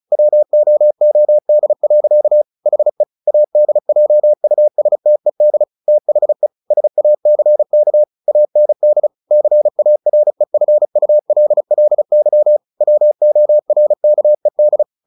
Morse code, heard as a keyed tone, about 34 wpm, 600 Hz. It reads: WOOD. HE ADJUSTED THE SACK AND CAREFULLY WORKED
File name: the old man and the sea / the old man and the sea 1 35wpm